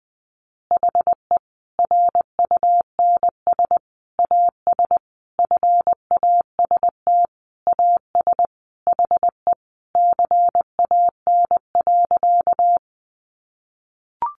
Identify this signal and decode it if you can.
Morse code, heard as a keyed tone, about 19 wpm, 700 Hz. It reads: HE RUNS AS FAST AS HE CAN.
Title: Additional Easy Phrases 20wpm